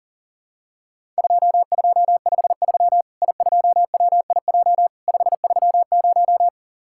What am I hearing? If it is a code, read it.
Morse code, heard as a keyed tone, about 40 wpm, 700 Hz. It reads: 2253 I2WIJ 530
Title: Worked All Europe DX Contest - QTC Exchange - 40wpm